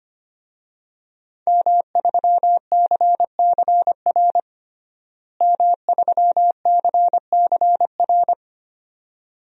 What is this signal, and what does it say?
Morse code, heard as a keyed tone, about 25 wpm, 700 Hz. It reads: M3CCR M3CCR
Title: DXCC-100 Call Signs - 5 Characters - Repeated 2x - 25wpm